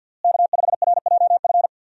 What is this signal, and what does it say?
Morse code, heard as a keyed tone, about 53 wpm, 700 Hz. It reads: K5RJF
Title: USA Call Signs - 5 Characters - Repeated 2x - 50wpm